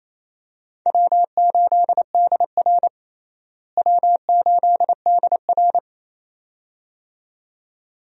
Morse code, heard as a keyed tone, about 28 wpm, 700 Hz. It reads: W8DR W8DR
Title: USA Call Signs - 4 Characters - Repeated 2x - 28wpm